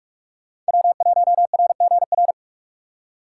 Morse code, heard as a keyed tone, about 56 wpm, 700 Hz. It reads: W1RGR
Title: Speed Racing - USA 1x3 Call Signs 45wpm